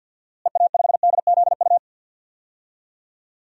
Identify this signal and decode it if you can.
Morse code, heard as a keyed tone, about 52 wpm, 700 Hz. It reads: EA5DZU
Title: DXCC-100 Call Signs - 6 Characters - Repeated 3x - 50wpm